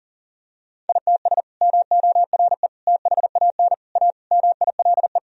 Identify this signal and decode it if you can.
Morse code, heard as a keyed tone, about 37 wpm, 700 Hz. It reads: ITS MORE THAN A MILE
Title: Sentences from Top 1000 Words - Review Part 1 - 40wpm